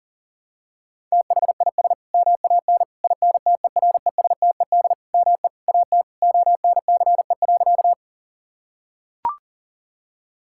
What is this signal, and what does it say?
Morse code, heard as a keyed tone, about 40 wpm, 700 Hz. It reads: THIS MAN INTERESTED ME AT ONCE.
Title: Sentences from Top 1000 Words - Review Part 6 - 40wpm